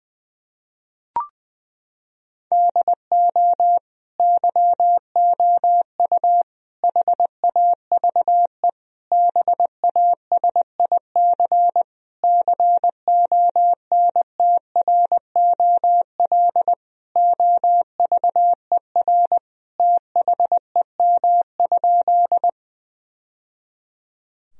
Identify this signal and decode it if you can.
Morse code, heard as a keyed tone, about 20 wpm, 700 Hz. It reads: DO YOU HAVE BASIC CONTROL OVER THEM?
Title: Sentences from Top 1000 Words - Review Part 3 - 20wpm